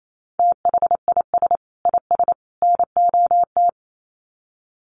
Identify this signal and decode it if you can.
Morse code, heard as a keyed tone, about 28 wpm, 700 Hz. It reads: THIS IS NOT
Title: Sets of 3 Words 28wpm